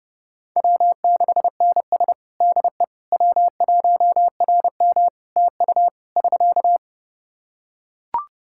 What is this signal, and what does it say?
Morse code, heard as a keyed tone, about 32 wpm, 700 Hz. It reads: W6NS DE W1RM TU <SK>
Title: Practice QSO - Exchange 4 - Ending - 30wpm